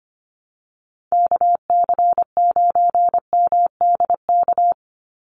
Morse code, heard as a keyed tone, about 25 wpm, 700 Hz. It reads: KC9MDK
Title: USA Call Signs 25wpm